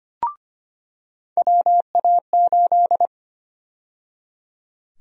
Morse code, heard as a keyed tone, about 25 wpm, 700 Hz. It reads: WA8
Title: USA Call Sign Prefixes - 25wpm